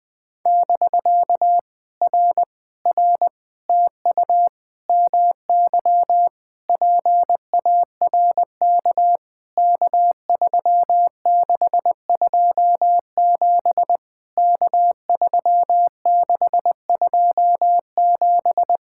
Morse code, heard as a keyed tone, about 19 wpm, 700 Hz. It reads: <BK> R R TU MY PARK K3627 K3627
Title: POTA QSO Practice - 20wpm